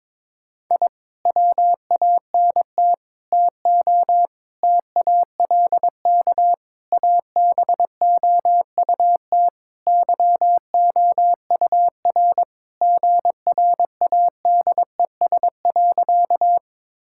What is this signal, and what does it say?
Morse code, heard as a keyed tone, about 22 wpm, 700 Hz. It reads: I WANT TO TALK ABOUT YOUR GRADES.
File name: practice / 22wpm / Sentences with 4-Word Phrases - 22wpm